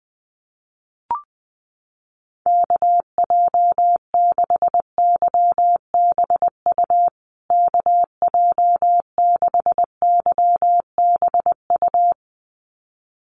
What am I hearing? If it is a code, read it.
Morse code, heard as a keyed tone, about 20 wpm, 700 Hz. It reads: KJ6YBU KJ6YBU
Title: USA Call Signs Repeated 2x 20wpm